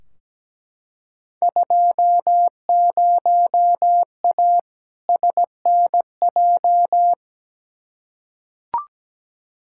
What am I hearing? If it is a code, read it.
Morse code, heard as a keyed tone, about 17 wpm, 700 Hz. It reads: 20A SNJ
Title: Field Day Exchanges Encore - 17wpm